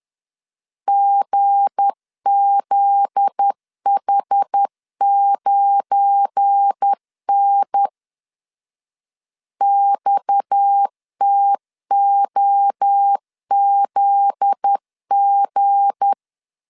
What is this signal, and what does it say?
Morse code, heard as a keyed tone, about 10 wpm, 800 Hz. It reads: GZH9N XTOZG
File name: LEZIONI / 9 2 10wpm 2736